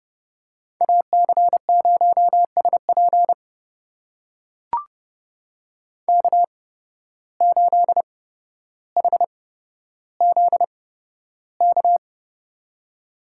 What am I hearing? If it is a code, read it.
Morse code, heard as a keyed tone, about 30 wpm, spaced as for 13 wpm, 700 Hz. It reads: AC0SP K8HZK
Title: USA Call Signs - 8x Character Spacing - 30wpm